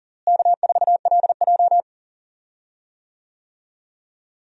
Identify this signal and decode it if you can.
Morse code, heard as a keyed tone, about 40 wpm, 700 Hz. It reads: K4LJ
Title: USA Call Signs - 4 Characters - Repeated 3x - 40wpm